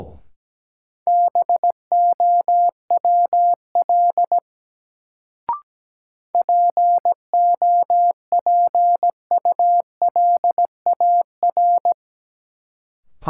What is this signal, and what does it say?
Morse code, heard as a keyed tone, about 17 wpm, 700 Hz. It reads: BOWL POPULAR
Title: Sentences from Top 1200 Words - New Word Drill Alt - 17wpm